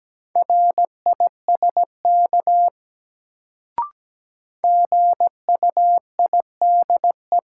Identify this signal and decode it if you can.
Morse code, heard as a keyed tone, about 17 wpm, 700 Hz. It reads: RISK GUIDE
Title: Sentences from Top 600 Words - New Word Drill - 17wpm